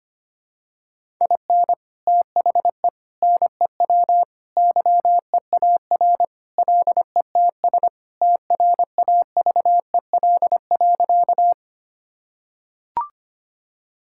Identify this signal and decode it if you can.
Morse code, heard as a keyed tone, about 25 wpm, 700 Hz. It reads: IN THE NEW YEAR LETS TRAVEL.
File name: practice / 25wpm / Sentences with 3-Word Phrases - 25wpm